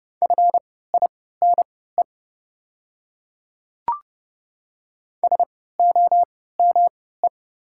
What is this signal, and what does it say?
Morse code, heard as a keyed tone, about 30 wpm, spaced as for 17 wpm, 690 Hz. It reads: FINE SOME
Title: 3x Character Spacing - Top 500 Words - 30wpm